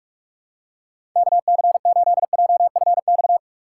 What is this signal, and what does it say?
Morse code, heard as a keyed tone, about 46 wpm, 700 Hz. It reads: KC8JFX
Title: USA Call Signs - 6 Characters - 2x Character Spacing - 45wpm